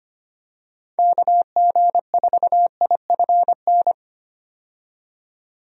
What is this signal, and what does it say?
Morse code, heard as a keyed tone, about 25 wpm, 700 Hz. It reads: KG4IFN
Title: USA Call Signs - 6 Characters - 25wpm